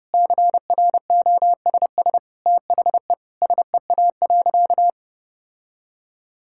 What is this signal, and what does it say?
Morse code, heard as a keyed tone, about 32 wpm, 700 Hz. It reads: CROSS THE SEA.
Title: Sentences from Top 1000 Words - Review Part 6 - 30wpm